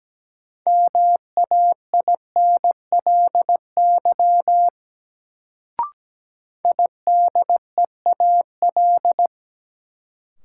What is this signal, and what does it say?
Morse code, heard as a keyed tone, about 17 wpm, 700 Hz. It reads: MAINLY IDEAL
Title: Sentences from Top 1400 Words - New Word Drill - 17wpm